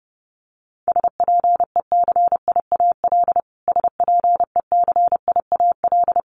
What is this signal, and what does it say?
Morse code, heard as a keyed tone, about 30 wpm, 700 Hz. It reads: SPECIAL SPECIAL
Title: Top 500 Words Repeated 30wpm